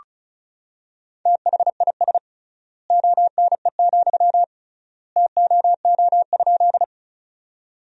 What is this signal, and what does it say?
Morse code, heard as a keyed tone, about 36 wpm, 700 Hz. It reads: THIS ONE, TOO?
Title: 3x Word-Spacing - Sentences from Top 200 Words - 35wpm